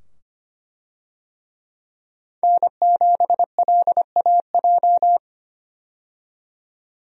Morse code, heard as a keyed tone, about 25 wpm, 700 Hz. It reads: N7LAJ
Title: USA Call Signs - 5 Characters - 25wpm